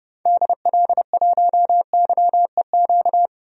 Morse code, heard as a keyed tone, about 30 wpm, 700 Hz. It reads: DL1YEQ
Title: DXCC-100 Call Signs - 6 Characters - 30wpm